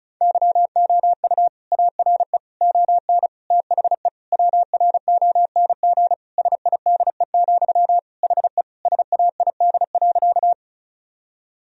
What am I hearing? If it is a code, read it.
Morse code, heard as a keyed tone, about 36 wpm, 700 Hz. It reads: YOU ARE ON THE WRONG SIDE, HE SAID.
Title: Sentences from Top 1000 Words - Review Part 1 - 35wpm